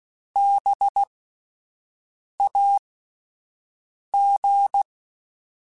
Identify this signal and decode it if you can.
Morse code, heard as a keyed tone, about 16 wpm, spaced as for 5 wpm, 800 Hz. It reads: BAG
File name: Alphabetical / Alphabetical mp3s / Bs @5